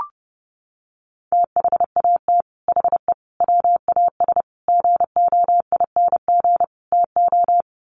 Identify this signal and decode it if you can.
Morse code, heard as a keyed tone, about 30 wpm, 700 Hz. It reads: THAT HE WAS GOING TO
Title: Sets of 5 Words 30wpm